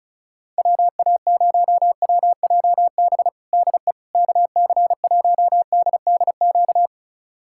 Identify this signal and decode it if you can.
Morse code, heard as a keyed tone, about 36 wpm, 700 Hz. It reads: WA0WJB DE KC1DDQ
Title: USA Callsign DE Callsign - 6 Characters - 35wpm